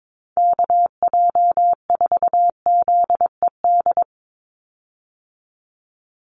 Morse code, heard as a keyed tone, about 22 wpm, 700 Hz. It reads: KJ4ZED
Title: USA Call Signs 22wpm